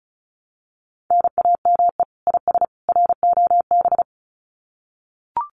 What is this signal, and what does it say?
Morse code, heard as a keyed tone, about 34 wpm, 700 Hz. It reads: NAME IS ROB
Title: Common QSO Phrases 35wpm